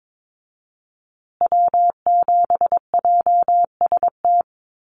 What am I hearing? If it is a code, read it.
Morse code, heard as a keyed tone, about 22 wpm, 700 Hz. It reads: W7JST
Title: USA Call Signs Repeated 2x 22wpm